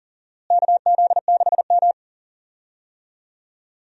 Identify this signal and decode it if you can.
Morse code, heard as a keyed tone, about 40 wpm, 700 Hz. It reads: KZ6M
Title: USA Call Signs - 4 Characters - Repeated 2x - 40wpm